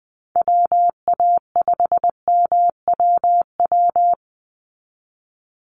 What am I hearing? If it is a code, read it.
Morse code, heard as a keyed tone, about 20 wpm, 700 Hz. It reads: WA5MWW
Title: USA Call Signs 20wpm